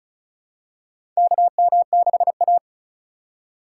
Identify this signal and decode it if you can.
Morse code, heard as a keyed tone, about 34 wpm, 700 Hz. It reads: KM6A
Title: USA Call Signs - 4 Characters - 4x Character Spacing - 35wpm